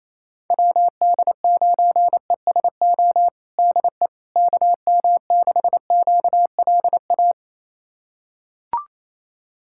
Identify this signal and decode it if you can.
Morse code, heard as a keyed tone, about 28 wpm, 700 Hz. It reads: WD9ESO DE KM6QLA
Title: USA Callsign DE Callsign 28wpm